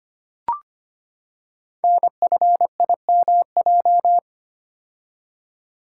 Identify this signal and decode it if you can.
Morse code, heard as a keyed tone, about 25 wpm, 700 Hz. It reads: NFIMJ
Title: Code Groups - 5 Letters - 25wpm